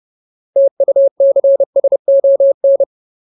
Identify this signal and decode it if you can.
Morse code, heard as a keyed tone, about 30 wpm, 550 Hz. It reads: TUCSON
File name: Ultimate Word List / 30WPM Ultimate